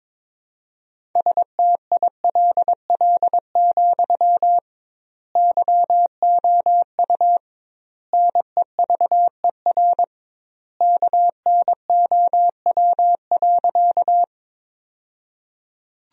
Morse code, heard as a keyed tone, about 22 wpm, 700 Hz. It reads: STILL, YOU NEVER KNOW.
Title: 2x Word-Spacing - Sentences from Top 300 Words - 22wpm